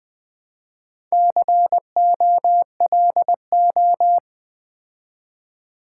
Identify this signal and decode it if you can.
Morse code, heard as a keyed tone, about 20 wpm, 700 Hz. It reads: COLO
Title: Build A Word - 2 Letters At A Time - 8-Letter Words - 20wpm